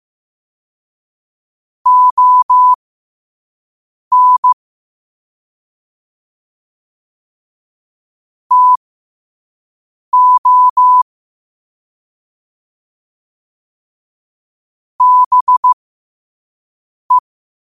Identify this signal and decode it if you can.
Morse code, heard as a keyed tone, about 15 wpm, spaced as for 5 wpm, 1000 Hz. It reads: ON TO BE
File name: 5wpm